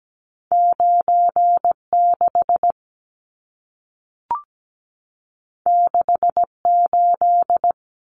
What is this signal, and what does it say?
Morse code, heard as a keyed tone, about 17 wpm, 700 Hz. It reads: 96 68
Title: Numbers - 2-Digit 17wpm